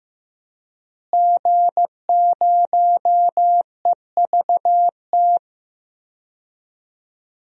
Morse code, heard as a keyed tone, about 15 wpm, 700 Hz. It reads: G0EVT
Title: DXCC-100 Call Signs - 5 Characters - Repeated 3x - 15wpm